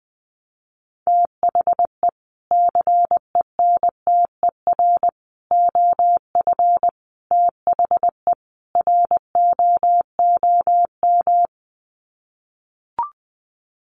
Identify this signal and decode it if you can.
Morse code, heard as a keyed tone, about 20 wpm, 700 Hz. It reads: THE CENTER OF THE ROOM
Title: Sets of 5 Words Encore 20wpm